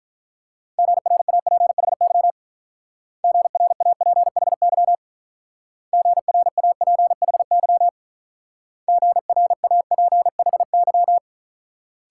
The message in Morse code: GRAPHY GRAPHY GRAPHY GRAPHY

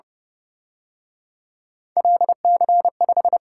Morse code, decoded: LC5

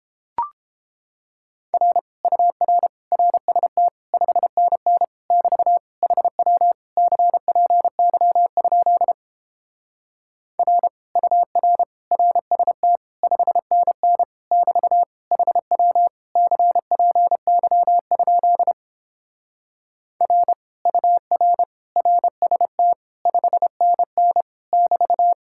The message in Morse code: R UR RST 5NN <BT> HW CPY? R UR RST 5NN <BT> HW CPY? R UR RST 5NN <BT>

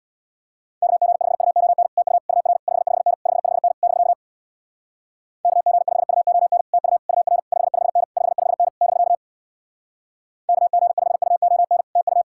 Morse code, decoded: KG5UON TU UR 55N 55N <BK> KG5UON TU UR 55N 55N <BK> KG5UON TU

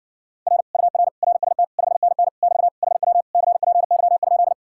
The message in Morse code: R UR RST 5NN <BT> HW CPY?

